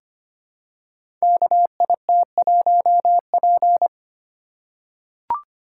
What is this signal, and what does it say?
Morse code, heard as a keyed tone, about 25 wpm, 700 Hz. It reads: KIT1P
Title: Code Groups - 5 Letter:Numbers - 25wpm